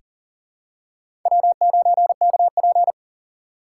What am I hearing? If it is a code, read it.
Morse code, heard as a keyed tone, about 38 wpm, 700 Hz. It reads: W9KP
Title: USA Call Signs - 4x Character Spacing - 40wpm